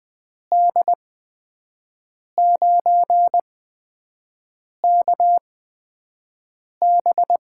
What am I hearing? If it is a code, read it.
Morse code, heard as a keyed tone, about 20 wpm, spaced as for 5 wpm, 700 Hz. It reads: D9KB
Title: USA Call Signs - 8x Character Spacing - 20wpm